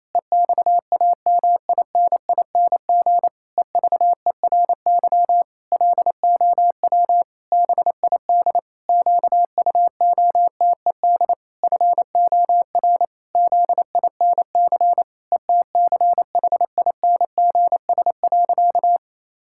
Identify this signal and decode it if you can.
Morse code, heard as a keyed tone, about 28 wpm, 700 Hz. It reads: EXAMINING EVERY LOW BID QUOTED FOR ZINC ETCHINGS.
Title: Pangram Challenge v2 - 28wpm